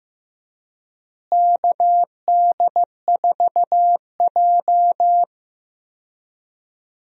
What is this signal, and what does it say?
Morse code, heard as a keyed tone, about 16 wpm, 700 Hz. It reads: KD4J
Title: USA Call Signs - 4 Characters - Repeated 2x - 15wpm